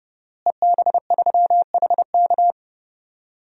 Morse code, heard as a keyed tone, about 32 wpm, 700 Hz. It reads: EB3HK
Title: DXCC-100 Call Signs - 5 Characters - Repeated 2x - 30wpm